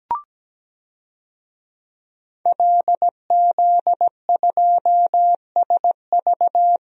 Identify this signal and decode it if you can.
Morse code, heard as a keyed tone, about 17 wpm, 700 Hz. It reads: LZ2SV
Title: DXCC-100 Call Signs - 5 Characters - 17wpm